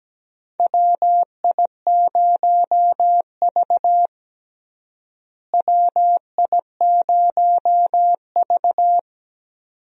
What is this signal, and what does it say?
Morse code, heard as a keyed tone, about 17 wpm, 700 Hz. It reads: WI0V WI0V
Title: USA Call Signs - 4 Characters - Repeated 2x - 17wpm